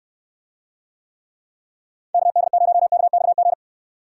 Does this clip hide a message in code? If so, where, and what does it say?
Yes: Morse code from 2.1 to 3.5 s, heard as a keyed tone, about 69 wpm, 700 Hz. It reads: KD0KXG